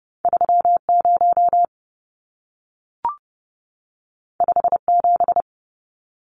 Morse code, heard as a keyed tone, about 30 wpm, 700 Hz. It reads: 30 57
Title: Numbers - 2-Digit 30wpm